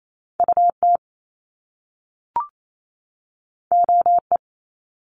Morse code, heard as a keyed tone, about 28 wpm, 700 Hz. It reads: UT OE
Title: Top Two Letter Combinations in Words 28wpm